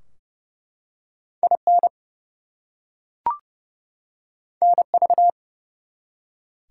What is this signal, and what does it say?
Morse code, heard as a keyed tone, about 30 wpm, 700 Hz. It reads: IN NV
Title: US States and Canadian Provinces 30wpm